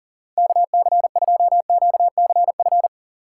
Morse code, heard as a keyed tone, about 40 wpm, 700 Hz. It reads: KC2QCF
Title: USA Call Signs - 4x Character Spacing - 40wpm